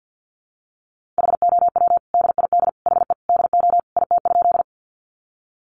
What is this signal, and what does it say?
Morse code, heard as a keyed tone, about 50 wpm, 700 Hz. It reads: HOW DID HE DO IT?
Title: Sentences from Top 100 Words 50wpm